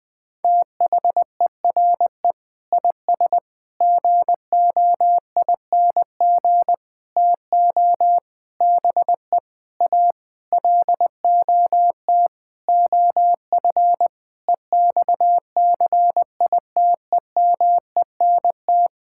Morse code, heard as a keyed tone, about 20 wpm, 700 Hz. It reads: THERE IS GOING TO BE A LOT OF EXCITEMENT
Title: Sentences with 7-Word Phrases - Spoken Beginning - 20wpm